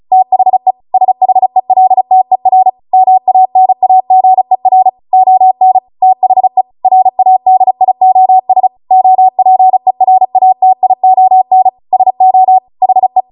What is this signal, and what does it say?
Morse code, heard as a keyed tone, about 36 wpm, 750 Hz. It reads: THE SHELTER MANAGER ON THE RADIOS OPERATION SO HE
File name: Morse / 35WPM